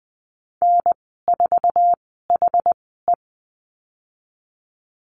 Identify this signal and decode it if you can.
Morse code, heard as a keyed tone, about 20 wpm, spaced as for 15 wpm, 700 Hz. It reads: N4HE
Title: 147 Lesson 24g - Letter y - Call Signs 20wpm